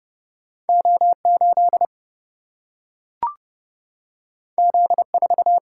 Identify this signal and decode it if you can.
Morse code, heard as a keyed tone, about 30 wpm, 700 Hz. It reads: O 8 Z 4
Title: Single-Letter Single-Number 30wpm